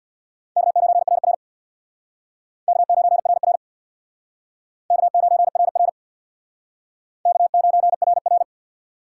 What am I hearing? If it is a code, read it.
Morse code, heard as a keyed tone, about 60 wpm, 690 Hz. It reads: K9RR K9RR K9RR K9RR
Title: Speed Racing - USA 1x2 Call Signs 50wpm